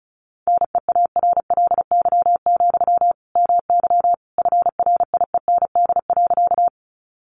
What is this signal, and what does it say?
Morse code, heard as a keyed tone, about 34 wpm, 700 Hz. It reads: NEARLY, MY FRIEND.
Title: Sentences from Top 1300 Words 35wpm